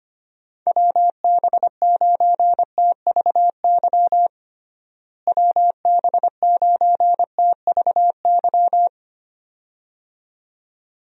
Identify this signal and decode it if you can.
Morse code, heard as a keyed tone, about 25 wpm, 700 Hz. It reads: WB9TVY WB9TVY
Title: USA Call Signs - 6 Characters - Repeated 2x - 25wpm